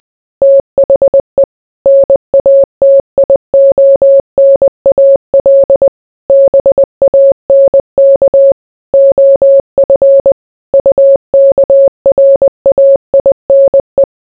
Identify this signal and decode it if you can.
Morse code, heard as a keyed tone, about 20 wpm, 550 Hz. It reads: THE NATIONAL BANK OF UKRAINE